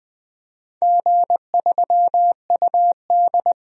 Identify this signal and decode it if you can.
Morse code, heard as a keyed tone, about 20 wpm, 700 Hz. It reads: G3UD
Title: DXCC-100 Call Signs - 4 Characters - 20wpm